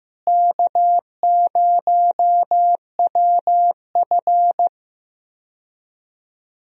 Morse code, heard as a keyed tone, about 15 wpm, 700 Hz. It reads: K0WF
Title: USA Call Signs - 4 Characters - Repeated 3x - 15wpm